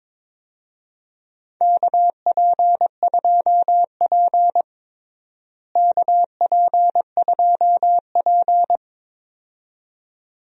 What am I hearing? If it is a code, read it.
Morse code, heard as a keyed tone, about 21 wpm, 700 Hz. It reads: KP2P KP2P
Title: DXCC-100 Call Signs - 4 Characters - Repeated 2x - 22wpm